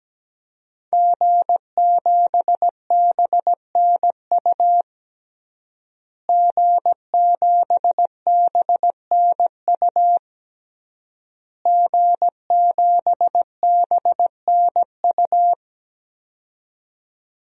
Morse code, heard as a keyed tone, about 17 wpm, 700 Hz. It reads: G7BNU G7BNU G7BNU
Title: DXCC-100 Call Signs - 5 Characters - Repeated 3x - 17wpm